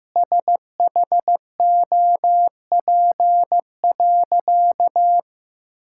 Morse code, heard as a keyed tone, about 15 wpm, 700 Hz. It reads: SHOP.
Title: Sentences from Top 1600 Words 15wpm